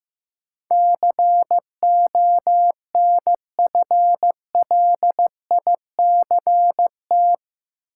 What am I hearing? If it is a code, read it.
Morse code, heard as a keyed tone, about 15 wpm, 700 Hz. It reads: CONFLICT
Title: Sentences from Top 1300 Words - New Word Drill Alt - 15wpm